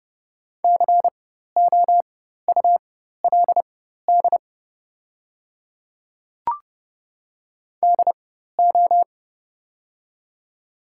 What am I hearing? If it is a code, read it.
Morse code, heard as a keyed tone, about 30 wpm, spaced as for 14 wpm, 700 Hz. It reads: COULD DO
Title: 4x Character Spacing - Top 300 Words - 30wpm